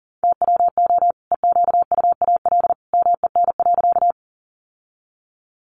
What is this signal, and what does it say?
Morse code, heard as a keyed tone, about 40 wpm, 700 Hz. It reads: TWO EQUAL MEN.
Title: Sentences from Top 1400 Words 40wpm